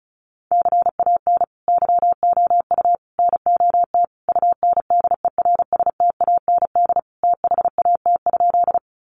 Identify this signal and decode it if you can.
Morse code, heard as a keyed tone, about 34 wpm, 700 Hz. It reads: CAN YOU NOT UNDERSTAND THAT?
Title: Sentences from Top 600 Words 35wpm